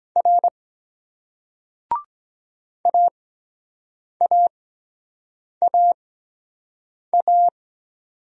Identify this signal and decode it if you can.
Morse code, heard as a keyed tone, about 23 wpm, spaced as for 7 wpm, 700 Hz. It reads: R AAAA